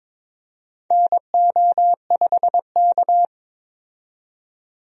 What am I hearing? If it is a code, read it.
Morse code, heard as a keyed tone, about 22 wpm, 700 Hz. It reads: NO5K
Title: USA Call Signs - 4 Characters - 22wpm